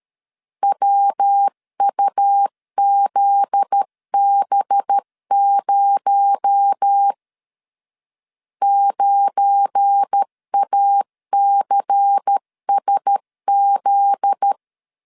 Morse code, heard as a keyed tone, about 13 wpm, 800 Hz. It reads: WUZB0 9ACSZ
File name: LEZIONI / 7 GF9Z / 7 GF9Z 12wpm 2147